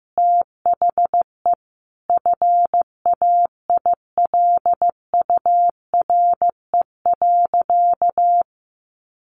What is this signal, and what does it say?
Morse code, heard as a keyed tone, about 14 wpm, 700 Hz. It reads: THE FAILURE.
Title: Sentences from Top 900 Words 15wpm